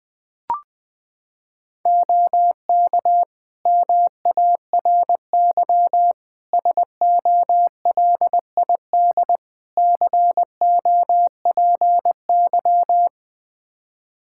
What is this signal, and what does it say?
Morse code, heard as a keyed tone, about 20 wpm, 700 Hz. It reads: OK MARY SOLID COPY
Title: Practice QSO - Exchange 3 - More Info - 20wpm